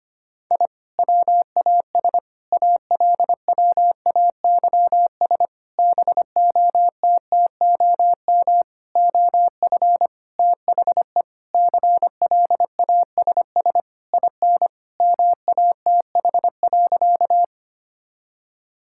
Morse code, heard as a keyed tone, about 25 wpm, 700 Hz. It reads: I WAS ALWAYS BOTTOM OF THE CLASS IN MATH.
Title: Sentences from Top 1000 Words - Review Part 6 - 25wpm